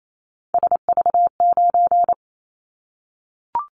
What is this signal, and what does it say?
Morse code, heard as a keyed tone, about 28 wpm, 700 Hz. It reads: SV9
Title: Callsign Prefixes for Top 100 DXCC - 28wpm